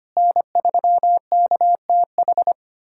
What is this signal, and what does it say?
Morse code, heard as a keyed tone, about 25 wpm, 700 Hz. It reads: N3KTH
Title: USA Callsign DE Callsign - 5 Characters - 25wpm